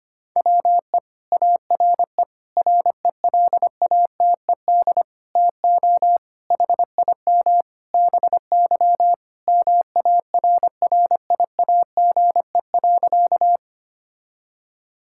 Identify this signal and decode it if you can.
Morse code, heard as a keyed tone, about 25 wpm, 700 Hz. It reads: WE ARE RELATED TO HIM BY MARRIAGE.
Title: Sentences from Top 1000 Words - Review Part 5 - 25wpm